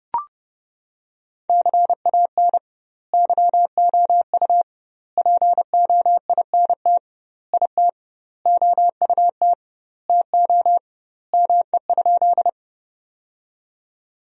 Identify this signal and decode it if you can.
Morse code, heard as a keyed tone, about 30 wpm, 700 Hz. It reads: CAN YOU POINT IT OUT TO ME?
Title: 2x Word-Spacing - Sentences from Top 200 Words - 30wpm